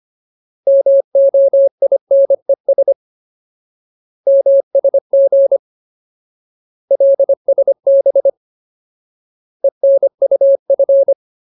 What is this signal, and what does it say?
Morse code, heard as a keyed tone, about 25 wpm, 550 Hz. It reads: MOINES MSG LSB ENUF
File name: Ultimate Word List / 25WPM Ultimate